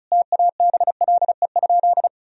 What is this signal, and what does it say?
Morse code, heard as a keyed tone, about 34 wpm, 700 Hz. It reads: TABLE?
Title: Sentences from Top 1000 Words - Review Part 4 - 35wpm